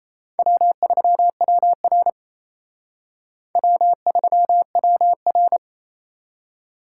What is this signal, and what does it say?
Morse code, heard as a keyed tone, about 35 wpm, 700 Hz. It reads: W3WR W3WR
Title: Speed Racing - USA 1x2 Call Signs 28wpm